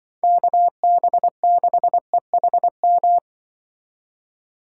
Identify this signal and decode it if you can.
Morse code, heard as a keyed tone, about 24 wpm, 700 Hz. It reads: KB6EHM
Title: Speed Racing - USA 2x3 Call Signs 20wpm